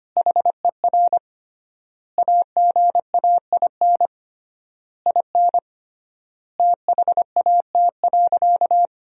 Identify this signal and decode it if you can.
Morse code, heard as a keyed tone, about 25 wpm, 700 Hz. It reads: HER AGAIN IN THAT.
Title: 3x Word-Spacing - Sentences from Top 200 Words - 25wpm